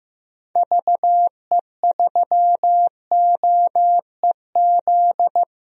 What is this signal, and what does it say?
Morse code, heard as a keyed tone, about 16 wpm, 700 Hz. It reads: VE3OEZ
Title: DXCC-100 Call Signs - 6 Characters - 15wpm